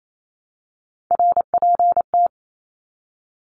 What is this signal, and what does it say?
Morse code, heard as a keyed tone, about 28 wpm, 700 Hz. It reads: RPT
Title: Abbreviations Repeated 3x - 28wpm